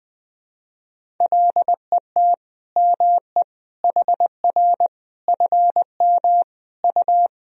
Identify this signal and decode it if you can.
Morse code, heard as a keyed tone, about 20 wpm, 700 Hz. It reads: LET ME HR FM U
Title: Sentences from Top 400 Words and Abbreviations 20wpm